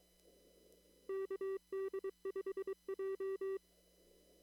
Morse code, heard as a keyed tone, about 22 wpm, 390 Hz. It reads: KD5J
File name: CW / kt5j wpx cw-op-k5pi